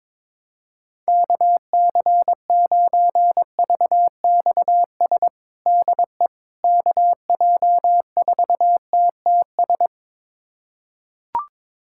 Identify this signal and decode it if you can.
Morse code, heard as a keyed tone, about 23 wpm, 700 Hz. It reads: KC9VXS DE KJ4TTS
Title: USA Callsign DE Callsign - 6 Characters - 22wpm